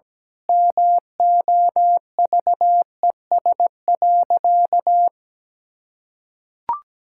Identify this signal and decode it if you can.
Morse code, heard as a keyed tone, about 17 wpm, 700 Hz. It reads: MOVES.
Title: Pangram Challenge v2 - 17wpm